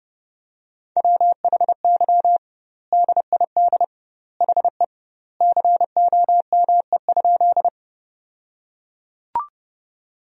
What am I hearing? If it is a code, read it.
Morse code, heard as a keyed tone, about 32 wpm, 700 Hz. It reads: WHY DID HE COME?
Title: 2x Word-Spacing - Sentences from Top 200 Words - 30wpm